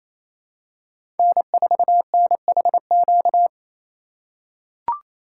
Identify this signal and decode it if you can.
Morse code, heard as a keyed tone, about 27 wpm, 700 Hz. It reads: N4NHQ
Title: USA Call Signs - 2x Character Spacing - 28wpm